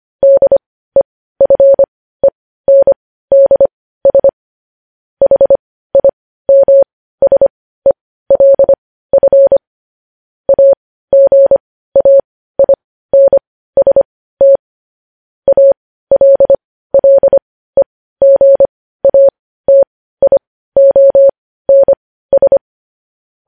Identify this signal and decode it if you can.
Morse code, heard as a keyed tone, about 25 wpm, spaced as for 15 wpm, 550 Hz. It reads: DEFENDS HIMSELF AGAINST ALLEGATIONS
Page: News headlines at 15